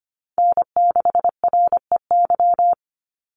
Morse code, heard as a keyed tone, about 25 wpm, 700 Hz. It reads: N6REY
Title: USA Call Signs 25wpm